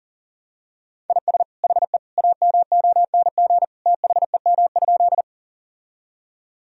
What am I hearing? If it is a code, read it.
Morse code, heard as a keyed tone, about 37 wpm, 700 Hz. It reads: IS HE AMONG THEM?
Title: Sentences from Top 1000 Words - Review Part 3 - 40wpm